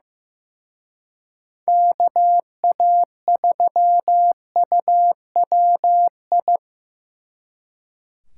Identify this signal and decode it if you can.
Morse code, heard as a keyed tone, about 18 wpm, 700 Hz. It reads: KA3UWI
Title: Speed Racing - USA 2x3 Call Signs 15wpm